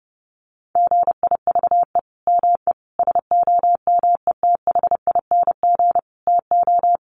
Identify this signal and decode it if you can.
Morse code, heard as a keyed tone, about 30 wpm, 700 Hz. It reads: GIVE ME SOMETHING TO
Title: Sentences from Top 1500 Words 30wpm